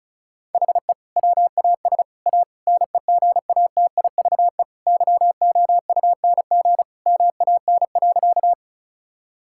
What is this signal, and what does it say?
Morse code, heard as a keyed tone, about 34 wpm, 700 Hz. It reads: HE WAS A NEGATIVE YOUNG MAN.
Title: Sentences from Top 1000 Words - Review Part 6 - 35wpm